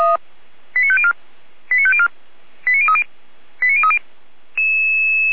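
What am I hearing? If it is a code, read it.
Five-tone selective call (ZVEI), heard as quick runs of five tones, tones of about 70 ms. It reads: 89504 89504 89030 89030
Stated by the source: ZVEI five-tone selective call received on an SDR.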